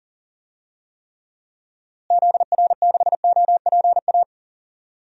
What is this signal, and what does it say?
Morse code, heard as a keyed tone, about 40 wpm, 700 Hz. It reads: ZR6OPA
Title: DXCC-100 Call Signs - 6 Characters - Repeated 2x - 40wpm